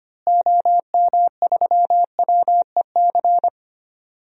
Morse code, heard as a keyed tone, about 25 wpm, 700 Hz. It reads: OM3WEC
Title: DXCC-100 Call Signs - 6 Characters - Repeated 2x - 25wpm